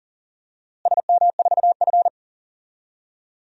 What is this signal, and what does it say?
Morse code, heard as a keyed tone, about 40 wpm, 700 Hz. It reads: SM4F
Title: DXCC-100 Call Signs - 4 Characters - Repeated 3x - 40wpm